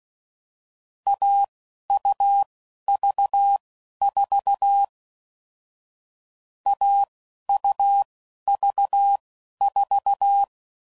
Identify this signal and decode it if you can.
Morse code, heard as a keyed tone, about 16 wpm, spaced as for 11 wpm, 800 Hz. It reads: AUV4 AUV4
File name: Combinations / Difficult Combos / Difficult Combos